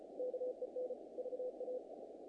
Morse code, heard as a keyed tone, about 34 wpm, 510 Hz, with noise. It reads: KEN VA
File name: contests / naqp-cw-k5tr-op-k5pi- -1800